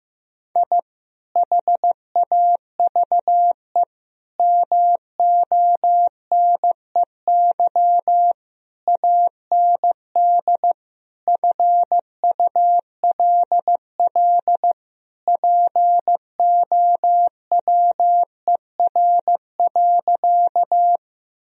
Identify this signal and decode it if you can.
Morse code, heard as a keyed tone, about 15 wpm, 700 Hz. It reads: I HAVE MONEY AND FULL POWER.
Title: Sentences from Top 1000 Words - Review Part 3 - 15wpm